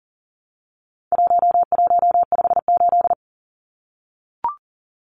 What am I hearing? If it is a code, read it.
Morse code, heard as a keyed tone, about 40 wpm, 700 Hz. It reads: 1158
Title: Numbers - 4-Digit 40wpm